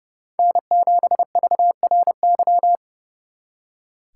Morse code, heard as a keyed tone, about 36 wpm, 700 Hz. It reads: N7VRY
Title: Speed Racing - USA 1x3 Call Signs 30wpm